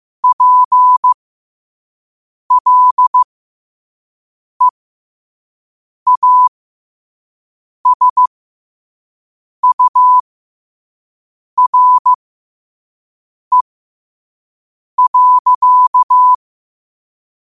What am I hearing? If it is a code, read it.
Morse code, heard as a keyed tone, about 15 wpm, spaced as for 5 wpm, 1000 Hz. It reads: PLEASURE.
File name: qotdfiles / 5wpm